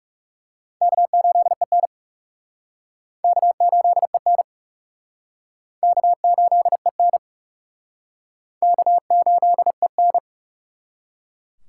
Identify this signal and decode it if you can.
Morse code, heard as a keyed tone, about 36 wpm, 700 Hz. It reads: K8EN K8EN K8EN K8EN
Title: Speed Racing - USA 1x2 Call Signs 30wpm